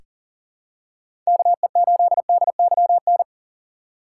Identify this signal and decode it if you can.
Morse code, heard as a keyed tone, about 40 wpm, 700 Hz. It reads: KE8DYN
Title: USA Call Signs - 6 Characters - 4x Character Spacing - 40wpm